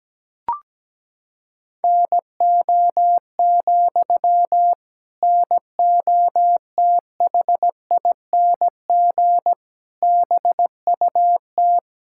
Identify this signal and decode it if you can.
Morse code, heard as a keyed tone, about 17 wpm, 700 Hz. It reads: NO, NOTHING BUT
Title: Sentences from Top 1600 Words 17wpm